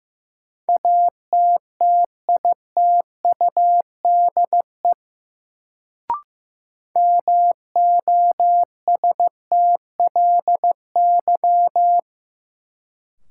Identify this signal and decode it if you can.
Morse code, heard as a keyed tone, about 15 wpm, 700 Hz. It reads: ATTITUDE MOSTLY
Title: Sentences from Top 1300 Words - New Word Drill - 15wpm